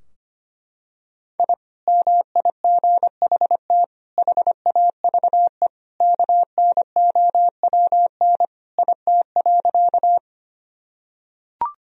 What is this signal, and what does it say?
Morse code, heard as a keyed tone, about 25 wpm, 700 Hz. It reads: I MIGHT HAVE KNOWN IT.
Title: Sentences from Top 1000 Words - Review Part 3 - 25wpm